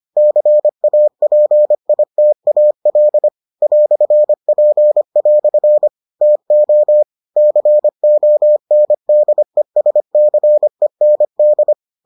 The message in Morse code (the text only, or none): CAPITAL "P" TO CONDESCEND